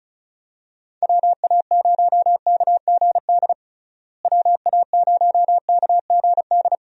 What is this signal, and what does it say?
Morse code, heard as a keyed tone, about 36 wpm, 700 Hz. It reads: WA0KGD WA0KGD
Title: USA Call Signs - 6 Characters - Repeated 2x - 35wpm